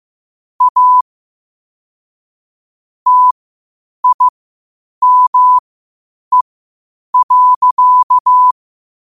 Morse code, heard as a keyed tone, about 15 wpm, spaced as for 8 wpm, 1000 Hz. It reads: A TIME.
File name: qotdfiles / 8wpm